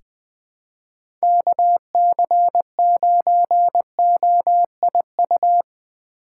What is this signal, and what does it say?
Morse code, heard as a keyed tone, about 20 wpm, 700 Hz. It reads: KC9OIU
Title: USA Call Signs - 6 Characters - 4x Character Spacing - 20wpm